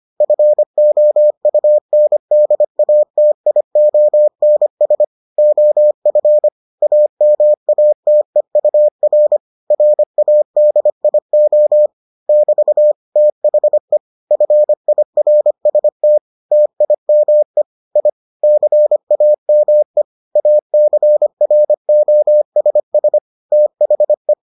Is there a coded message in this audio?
Yes: Morse code from 0.2 to 24.3 s, heard as a keyed tone, about 25 wpm, 600 Hz. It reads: FOUNDATIONS OF AMATEUR RADIO <BT> THE FIRST TIME I CAME ACROSS THE